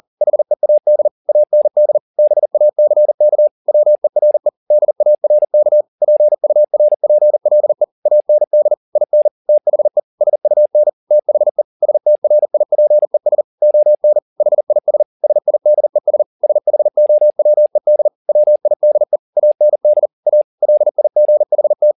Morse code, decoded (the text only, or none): HEAD AND BACK WERE DARK PURPLE AND IN THE SUN THE STRIPES ON HIS SIDES SHOWED WIDE AND A LIGHT